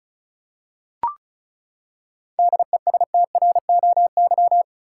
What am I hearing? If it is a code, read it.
Morse code, heard as a keyed tone, about 34 wpm, 700 Hz. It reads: DESTROY
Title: Seven Letter Words - 35wpm